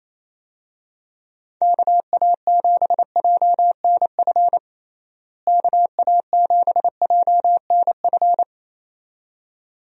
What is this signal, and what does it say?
Morse code, heard as a keyed tone, about 28 wpm, 700 Hz. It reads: KA7JNF KA7JNF
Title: USA Call Signs - 6 Characters - Repeated 2x - 28wpm